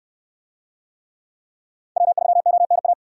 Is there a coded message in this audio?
Yes: Morse code from 2.0 to 2.9 s, heard as a keyed tone, about 67 wpm, 700 Hz. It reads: W3ONA